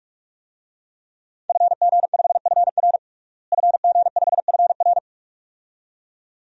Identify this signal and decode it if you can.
Morse code, heard as a keyed tone, about 46 wpm, 700 Hz. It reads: FG5FR FG5FR
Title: DXCC-100 Call Signs - 5 Characters - Repeated 2x - 45wpm